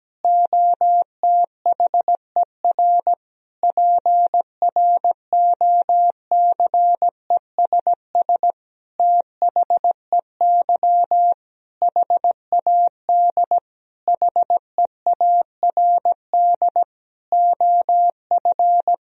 Morse code, heard as a keyed tone, about 17 wpm, 700 Hz. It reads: OTHER PROCESS THEY HAD HEARD OF
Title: Sentences from Top 1000 Words - Review Part 3 - 17wpm